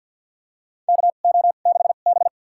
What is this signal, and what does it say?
Morse code, heard as a keyed tone, about 50 wpm, spaced as for 36 wpm, 700 Hz. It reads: KO6B
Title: USA Call Signs - 4 Characters - 2x Character Spacing - 50wpm